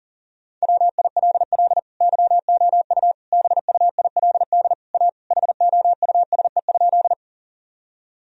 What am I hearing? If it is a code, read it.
Morse code, heard as a keyed tone, about 40 wpm, 700 Hz. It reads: WILL YOU BUILD A HOUSE?
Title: Sentences from Top 1000 Words - Review Part 1 - 40wpm